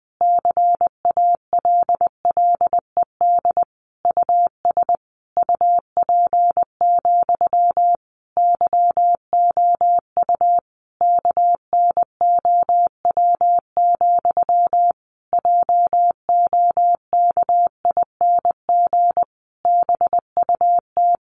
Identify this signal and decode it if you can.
Morse code, heard as a keyed tone, about 20 wpm, 700 Hz. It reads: CALLED US UP, YOU KNOW, JOKING BUT
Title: A Sound Of Thunder 20wpm